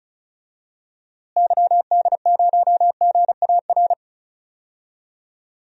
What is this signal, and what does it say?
Morse code, heard as a keyed tone, about 34 wpm, 700 Hz. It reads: YD0GAR
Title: DXCC-100 Call Signs - 6 Characters - Repeated 2x - 35wpm